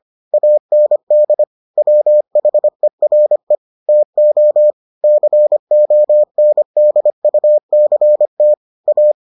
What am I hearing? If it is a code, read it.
Morse code, heard as a keyed tone, about 25 wpm, 600 Hz. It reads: AND WHERE TO CONDUCT A